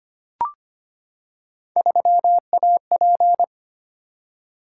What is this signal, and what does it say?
Morse code, heard as a keyed tone, about 25 wpm, 700 Hz. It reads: 3AP
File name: Three Letter-Number Combinations - 25wpm